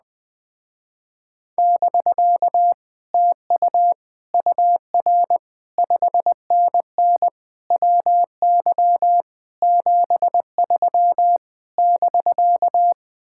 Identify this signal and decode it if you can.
Morse code, heard as a keyed tone, about 20 wpm, 700 Hz. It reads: <BK> TU UR 5NN WY 73 <BK>
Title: Speed-Racing Common QSO Phrases - 17wpm